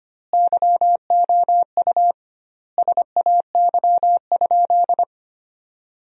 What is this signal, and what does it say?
Morse code, heard as a keyed tone, about 25 wpm, 700 Hz. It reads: YOU SAY?
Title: 2x Word-Spacing - Sentences from Top 200 Words - 25wpm